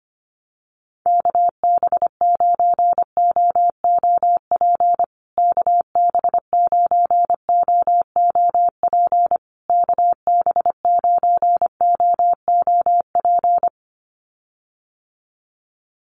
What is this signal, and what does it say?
Morse code, heard as a keyed tone, about 25 wpm, 700 Hz. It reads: KB9OOP KB9OOP KB9OOP
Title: USA Call Signs Repeated 3x 25wpm